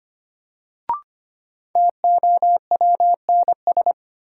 Morse code, heard as a keyed tone, about 25 wpm, 700 Hz. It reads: TOWNS
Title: Sentences from Top 1000 Words - Review Part 6 - New Word Drill - 25wpm